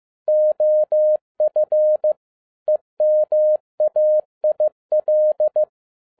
Morse code, heard as a keyed tone, about 15 wpm, 610 Hz. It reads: OF EMAIL
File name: 2011-07-mp3 / n37 THE 3.11 2 15wpm